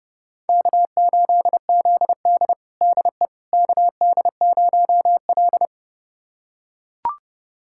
Morse code, heard as a keyed tone, about 30 wpm, 700 Hz. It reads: K8ZD DE KD0L
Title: USA Callsign DE Callsign - 4 Characters - 30wpm